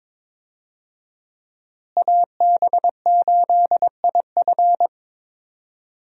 Morse code, heard as a keyed tone, about 22 wpm, 700 Hz. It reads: AB8IF
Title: USA Call Signs - 5 Characters - Repeated 2x - 22wpm